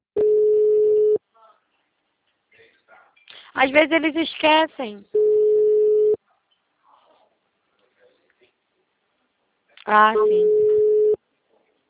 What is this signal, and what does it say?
Ringback tone (Europe, 425 Hz), with speech.
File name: out- -2016- - - .63370